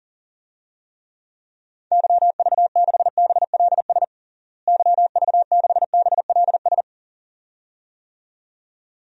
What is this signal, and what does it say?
Morse code, heard as a keyed tone, about 40 wpm, 700 Hz. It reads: YV6BLS YV6BLS
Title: DXCC-100 Call Signs - 6 Characters - Repeated 2x - 40wpm